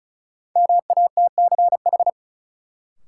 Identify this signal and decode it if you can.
Morse code, heard as a keyed tone, about 36 wpm, 700 Hz. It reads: MATCH
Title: Sentences from Top 1600 Words - New Word Drill - 35wpm